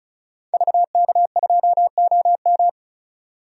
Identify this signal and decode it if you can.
Morse code, heard as a keyed tone, about 36 wpm, 700 Hz. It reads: VK2OM
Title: DXCC-100 Call Signs - 5 Characters - 35wpm